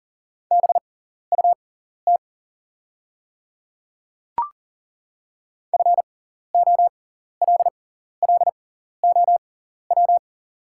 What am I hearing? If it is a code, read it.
Morse code, heard as a keyed tone, about 37 wpm, spaced as for 14 wpm, 690 Hz. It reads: BUT FOLLOW
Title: 6x Character Spacing - Top 300 Words - 40wpm